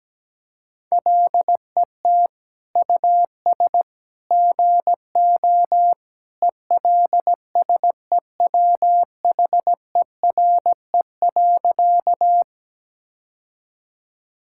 Morse code, heard as a keyed tone, about 17 wpm, 700 Hz. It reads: LET US GO ELSEWHERE.
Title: Sentences from Top 1600 Words 17wpm